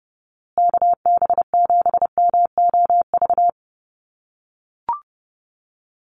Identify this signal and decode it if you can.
Morse code, heard as a keyed tone, about 30 wpm, 700 Hz. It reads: KB7MOV
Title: USA Call Signs Repeated 3x 30wpm